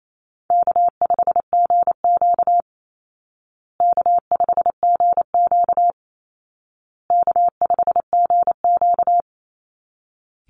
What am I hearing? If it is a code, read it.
Morse code, heard as a keyed tone, about 28 wpm, 700 Hz. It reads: K5GQ K5GQ K5GQ
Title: Easy Call Signs Repeated 3x - 28wpm